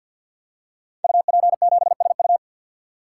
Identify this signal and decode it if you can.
Morse code, heard as a keyed tone, about 50 wpm, 700 Hz. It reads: UP7SU
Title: Code Groups - 5 Letter:Numbers - 50wpm